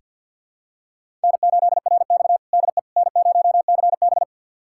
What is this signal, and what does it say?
Morse code, heard as a keyed tone, about 47 wpm, 700 Hz. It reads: N8RX DE N0CB
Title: USA Callsign DE Callsign - 4 Characters - 50wpm